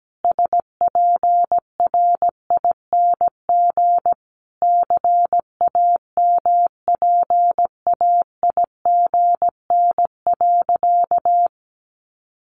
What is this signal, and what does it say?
Morse code, heard as a keyed tone, about 17 wpm, 700 Hz. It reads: SPRING CAMPAIGN.
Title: Sentences from Top 1200 Words 17wpm